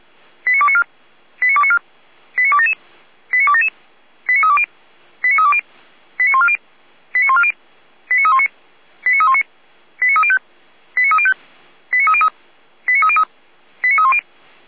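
Five-tone selective call (ZVEI), heard as quick runs of five tones, tones of about 70 ms. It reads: 89294 89294 89277 89277 89320 89320 89140 89140 89319 89319 89305 89305 89303 89303 89310
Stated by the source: ZVEI five-tone selective call received on SDR.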